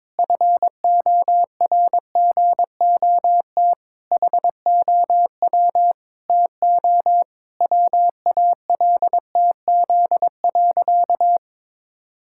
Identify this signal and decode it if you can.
Morse code, heard as a keyed tone, about 22 wpm, 700 Hz. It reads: FORGOT HOW TO WALTZ.
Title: Pangram Challenge v2 - 22wpm